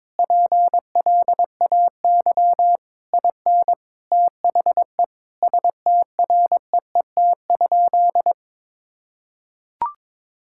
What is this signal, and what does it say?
Morse code, heard as a keyed tone, about 22 wpm, 700 Hz. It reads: PLAY IN THE STREET?
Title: Sentences from Top 1600 Words 22wpm